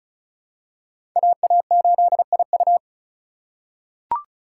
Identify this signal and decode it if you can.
Morse code, heard as a keyed tone, about 34 wpm, 700 Hz. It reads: AA8IU
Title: USA Call Signs - 5 Characters - 2x Character Spacing - 35wpm